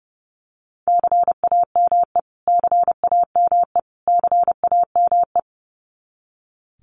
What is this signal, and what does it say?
Morse code, heard as a keyed tone, about 30 wpm, 700 Hz. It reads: CAME CAME CAME
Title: Top 200 Words Repeated 30wpm